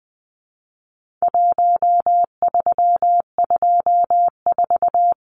Morse code, heard as a keyed tone, about 20 wpm, 700 Hz. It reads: 1324
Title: Numbers - Contest Style - 4-Digit 20wpm